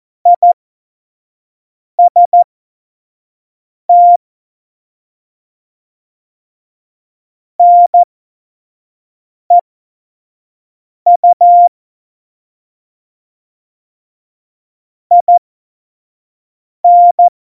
Morse code, heard as a keyed tone, about 14 wpm, spaced as for 5 wpm, 700 Hz. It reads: IST NEU IN
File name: Morsekurs / Witze / Witz 04. Juni 2019, 25 F70